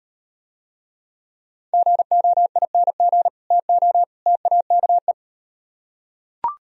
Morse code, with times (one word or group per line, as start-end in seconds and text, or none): 1.7-3.3 GOING
3.5-4.0 TO
4.3-5.1 TAKE